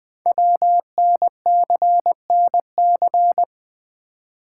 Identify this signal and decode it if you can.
Morse code, heard as a keyed tone, about 20 wpm, 700 Hz. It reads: WNCNC
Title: Code Groups - 5 Letters - 20wpm